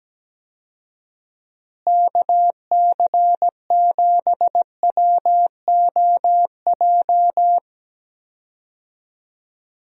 Morse code, heard as a keyed tone, about 17 wpm, 700 Hz. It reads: KC7WOJ
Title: USA Call Signs - 6 Characters - Repeated 3x - 17wpm